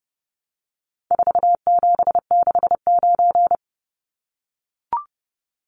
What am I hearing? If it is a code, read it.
Morse code, heard as a keyed tone, about 30 wpm, 700 Hz. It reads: 4769
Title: Numbers - 4-Digit 30wpm